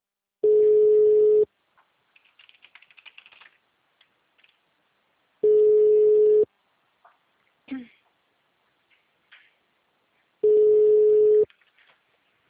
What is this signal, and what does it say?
Ringback tone (Europe, 425 Hz), with speech.